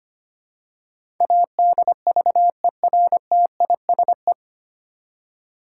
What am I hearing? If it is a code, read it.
Morse code, heard as a keyed tone, about 25 wpm, 700 Hz. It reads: ADVERTISE
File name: Nine Letter Words - 25wpm